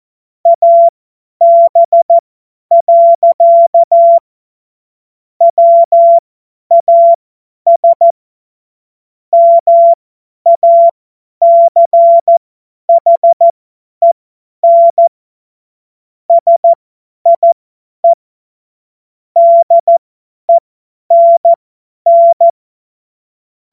Morse code, heard as a keyed tone, about 14 wpm, spaced as for 10 wpm, 680 Hz. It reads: AB. WAS MACHEN SIE DENN
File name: Morsekurs / Witze / Witz 01. Juni 2021, 50 F70